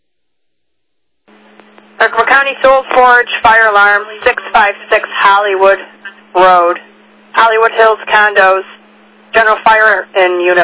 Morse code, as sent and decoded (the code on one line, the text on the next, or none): none